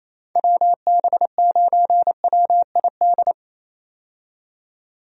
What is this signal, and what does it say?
Morse code, heard as a keyed tone, about 28 wpm, 700 Hz. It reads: WB9WID
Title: USA Call Signs - 6 Characters - 28wpm